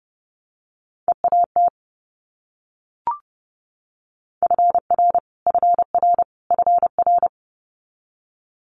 Morse code, heard as a keyed tone, about 32 wpm, 700 Hz. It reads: EAT FR FR FR
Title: Multi-letter Phonemes Repeated 3x 30wpm